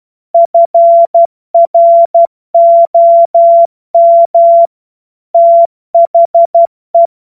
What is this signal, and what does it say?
Morse code, heard as a keyed tone, about 12 wpm, 680 Hz. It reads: FROM THE
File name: Morsekurs / Witze